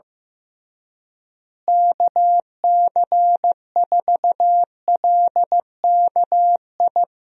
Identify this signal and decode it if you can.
Morse code, heard as a keyed tone, about 15 wpm, 700 Hz. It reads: KC4LKI
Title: USA Call Signs - 6 Characters - Repeated 3x - 15wpm